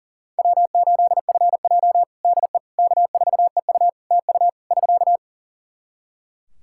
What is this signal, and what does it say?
Morse code, heard as a keyed tone, about 40 wpm, 700 Hz. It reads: W8FJ DE K4EU TU <SK>
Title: Practice QSO - Exchange 4 - Ending - 40wpm